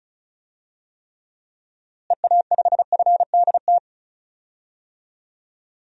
Morse code, heard as a keyed tone, about 34 wpm, 700 Hz. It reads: EA5FDT
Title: DXCC-100 Call Signs - 6 Characters - Repeated 2x - 35wpm